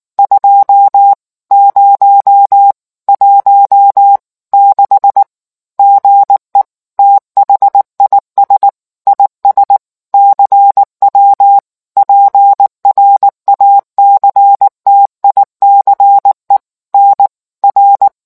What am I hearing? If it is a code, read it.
Morse code, heard as a keyed tone, about 19 wpm, 800 Hz. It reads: 2016 GE THIS IS CW PRACTICE NR